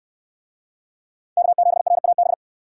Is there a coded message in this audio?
Yes: Morse code from 1.4 to 2.3 s, heard as a keyed tone, about 69 wpm, 700 Hz. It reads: K7RAB